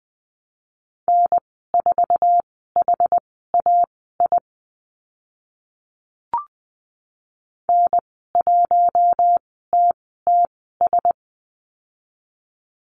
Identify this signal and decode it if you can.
Morse code, heard as a keyed tone, about 20 wpm, spaced as for 14 wpm, 700 Hz. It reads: N4HAI N1TTS
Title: 062 Lesson 11g - Letter h - Call Signs 20wpm